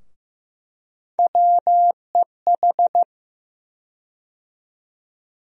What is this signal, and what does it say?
Morse code, heard as a keyed tone, about 15 wpm, 700 Hz. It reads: WEH
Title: USA Call Sign Postfixes - 3 Letters 15wpm